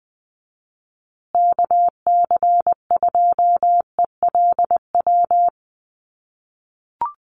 Morse code, heard as a keyed tone, about 20 wpm, 700 Hz. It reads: KC2ELW
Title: USA Call Signs Repeated 2x 20wpm